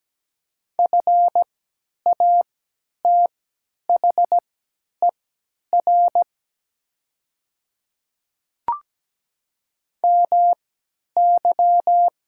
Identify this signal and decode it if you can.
Morse code, heard as a keyed tone, about 17 wpm, spaced as for 10 wpm, 700 Hz. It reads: FATHER MY
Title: 3x Character Spacing - Top 200 Words - 17wpm